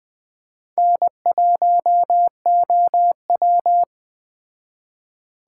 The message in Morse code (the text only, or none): N1OW